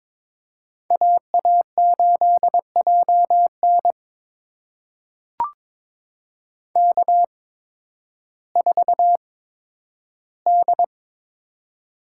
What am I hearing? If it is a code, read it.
Morse code, heard as a keyed tone, about 22 wpm, spaced as for 12 wpm, 700 Hz. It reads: AA8JN K4D
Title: USA Call Signs - 5 Characters - 8x Character Spacing - 22wpm